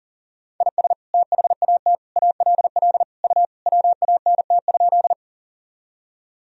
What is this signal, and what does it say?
Morse code, heard as a keyed tone, about 40 wpm, 700 Hz. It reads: IS THAT ALL U WANT?
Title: Sentences from Top 200 Words and Abbreviations 40wpm